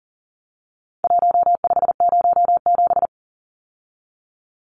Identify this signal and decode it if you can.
Morse code, heard as a keyed tone, about 40 wpm, 700 Hz. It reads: 1507
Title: Numbers - Contest Style - 4-Digit 40wpm